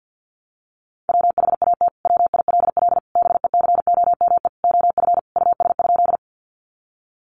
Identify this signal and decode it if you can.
Morse code, heard as a keyed tone, about 50 wpm, 700 Hz. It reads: WHAT WILL BECOME OF US?
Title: Sentences from Top 600 Words 50wpm